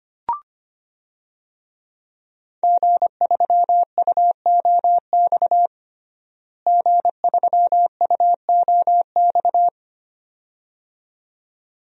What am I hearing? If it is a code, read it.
Morse code, heard as a keyed tone, about 25 wpm, 700 Hz. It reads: G3UOX G3UOX
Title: DXCC-100 Call Signs - 5 Characters - Repeated 2x - 25wpm